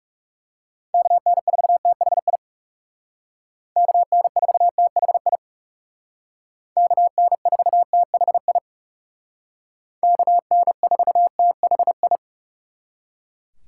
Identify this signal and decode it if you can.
Morse code, heard as a keyed tone, about 38 wpm, 700 Hz. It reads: KN4THI KN4THI KN4THI KN4THI